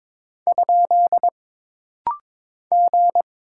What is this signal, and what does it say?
Morse code, heard as a keyed tone, about 22 wpm, 700 Hz. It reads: ? G
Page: This rapid-fire timing is geared to push your ICR (Instant Character Recognition) skills.